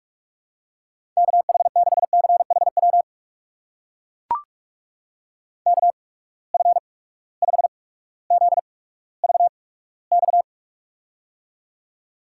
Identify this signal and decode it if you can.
Morse code, heard as a keyed tone, about 41 wpm, spaced as for 19 wpm, 700 Hz. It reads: KH6CHW KF5ZVX
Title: USA Call Signs - 6 Characters - 8x Character Spacing - 45wpm